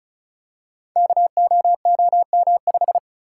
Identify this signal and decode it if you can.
Morse code, heard as a keyed tone, about 36 wpm, 700 Hz. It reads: KOOM5
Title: Code Groups - 5 Letter:Numbers - 35wpm